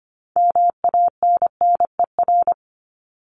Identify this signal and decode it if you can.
Morse code, heard as a keyed tone, about 25 wpm, 700 Hz. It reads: MANNER
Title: Top 1501-1750 Words 25wpm